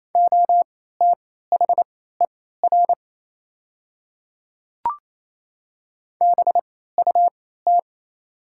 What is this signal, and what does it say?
Morse code, heard as a keyed tone, about 28 wpm, spaced as for 16 wpm, 700 Hz. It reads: OTHER BUT
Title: 3x Character Spacing - Top 100 Words - 28wpm